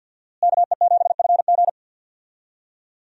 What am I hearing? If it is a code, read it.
Morse code, heard as a keyed tone, about 51 wpm, 700 Hz. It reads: KE7FG
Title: USA Call Signs - 5 Characters - Repeated 2x - 50wpm